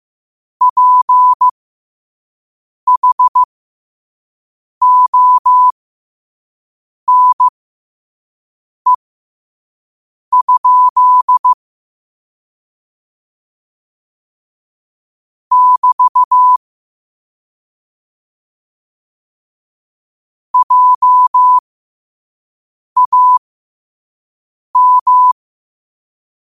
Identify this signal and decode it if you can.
Morse code, heard as a keyed tone, about 15 wpm, spaced as for 5 wpm, 1000 Hz. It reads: PHONE? <BT> JAM
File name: 2026-03-16-5wpm